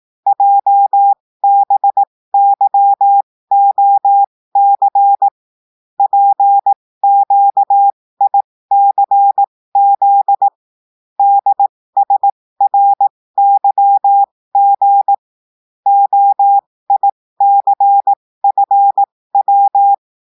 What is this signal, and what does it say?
Morse code, heard as a keyed tone, about 18 wpm, spaced as for 15 wpm, 800 Hz. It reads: JBYOC PQICZ DSRYG OICFW
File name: cours-du-lundi files / cours-1580-texte-chiffre-15-mpm